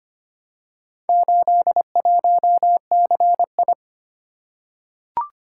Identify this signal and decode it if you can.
Morse code, heard as a keyed tone, about 25 wpm, 700 Hz. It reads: 81CI
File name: practice / 25wpm / Four Letter-Number Combinations - 25wpm